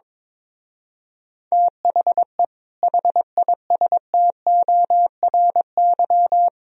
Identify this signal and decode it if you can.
Morse code, heard as a keyed tone, about 23 wpm, 700 Hz. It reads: THE HISTORY
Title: Speed-Racing - Sets of 3 Words 22wpm